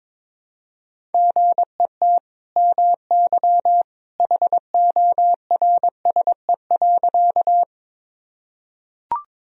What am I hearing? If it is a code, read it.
Morse code, heard as a keyed tone, about 22 wpm, 700 Hz. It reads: GET MY HORSE.
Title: Sentences from Top 1000 Words - Review Part 1 - 22wpm